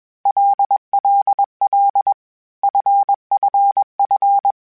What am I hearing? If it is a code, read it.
Morse code, heard as a keyed tone, about 21 wpm, 800 Hz. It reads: LLL FFF
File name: Basic Sets / Basic Letter Groups Collective7 / LFCYQ@20